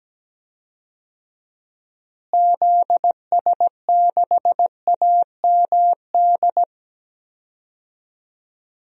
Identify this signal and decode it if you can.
Morse code, heard as a keyed tone, about 17 wpm, 700 Hz. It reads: ZS6AMD
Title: DXCC-100 Call Signs - 6 Characters - Repeated 2x - 17wpm